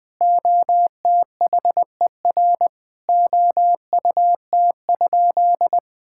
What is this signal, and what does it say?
Morse code, heard as a keyed tone, about 21 wpm, 700 Hz. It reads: OTHER OUT?
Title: Sentences from Top 1000 Words - Review Part 6 - 20wpm